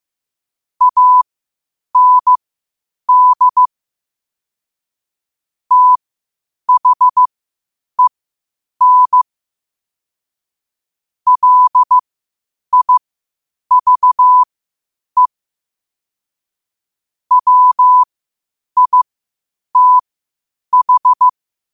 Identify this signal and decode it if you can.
Morse code, heard as a keyed tone, about 15 wpm, spaced as for 8 wpm, 1000 Hz. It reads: AND THEN LIVE WITH
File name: qotdfiles / 8wpm